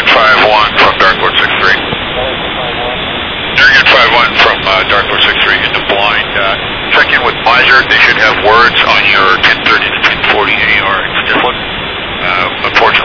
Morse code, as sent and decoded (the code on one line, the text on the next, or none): none